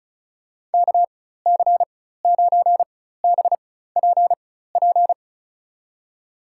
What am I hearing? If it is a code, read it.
Morse code, heard as a keyed tone, about 36 wpm, spaced as for 16 wpm, 700 Hz. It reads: KC9BPP
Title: USA Call Signs - 6 Characters - 4x Character Spacing - 35wpm